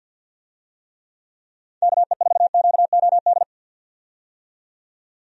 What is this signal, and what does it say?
Morse code, heard as a keyed tone, about 51 wpm, 700 Hz. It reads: KE4QOD
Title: USA Call Signs - 6 Characters - 50wpm